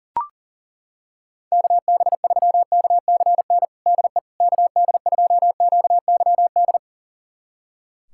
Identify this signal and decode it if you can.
Morse code, heard as a keyed tone, about 40 wpm, 700 Hz. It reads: KB3KCN DE KD2QYD
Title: USA Callsign DE Callsign - 6 Characters - 40wpm